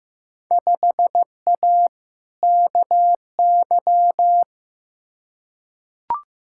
Morse code, heard as a keyed tone, about 15 wpm, 700 Hz. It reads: 5A KY
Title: Field Day Exchanges - 15wpm